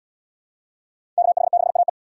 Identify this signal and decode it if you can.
Morse code, heard as a keyed tone, about 77 wpm, 700 Hz. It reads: KH6AE